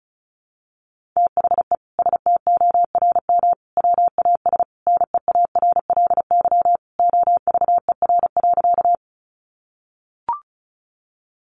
Morse code, heard as a keyed tone, about 34 wpm, 700 Hz. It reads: THE STORM WAS NEARLY OVER.
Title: Sentences from Top 1500 Words 35wpm